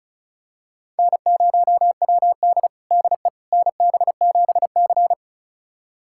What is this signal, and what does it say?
Morse code, heard as a keyed tone, about 34 wpm, 690 Hz. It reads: N0WD DE NB7C
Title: USA Callsign DE Callsign - 4 Characters - 35wpm